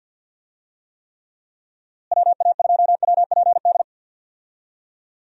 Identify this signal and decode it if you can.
Morse code, heard as a keyed tone, about 52 wpm, 700 Hz. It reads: WA2WPD
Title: USA Call Signs - 6 Characters - Repeated 2x - 50wpm